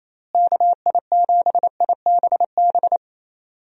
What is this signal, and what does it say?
Morse code, heard as a keyed tone, about 28 wpm, 700 Hz. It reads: KI7IBB
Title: USA Call Signs - 6 Characters - Repeated 2x - 28wpm